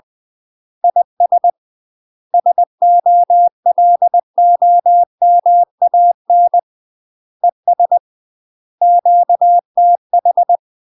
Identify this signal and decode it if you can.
Morse code, heard as a keyed tone, about 20 wpm, 700 Hz. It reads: IS SOLOMAN ES QTH
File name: UK MASTER QSOs TRAINING / 20WPM / UK MASTER QSOs 0046